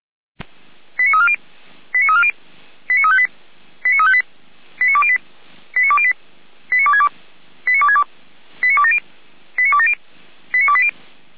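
Five-tone selective call (ZVEI), heard as quick runs of five tones, tones of about 70 ms. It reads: 89340 89340 89358 89358 89208 89208 89262 89262 89270 89270 89280
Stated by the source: ZVEI five-tone selective call received with an SDR.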